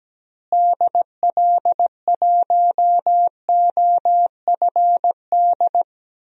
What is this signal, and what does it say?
Morse code, heard as a keyed tone, about 17 wpm, 700 Hz. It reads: DL1OFD
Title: DXCC-100 Call Signs - 6 Characters - Repeated 3x - 17wpm